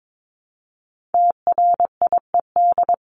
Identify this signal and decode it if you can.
Morse code, heard as a keyed tone, about 22 wpm, 700 Hz. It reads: TRIED
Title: ED Suffix - Top 100 Words with Base Words 22wpm